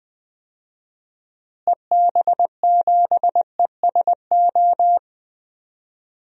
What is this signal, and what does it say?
Morse code, heard as a keyed tone, about 20 wpm, 690 Hz. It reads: EB7ESO
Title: DXCC-100 Call Signs - 6 Characters - Repeated 2x - 20wpm